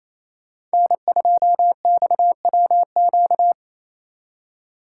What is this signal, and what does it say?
Morse code, heard as a keyed tone, about 26 wpm, 700 Hz. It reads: N2XWQ
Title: Speed Racing - USA 1x3 Call Signs 28wpm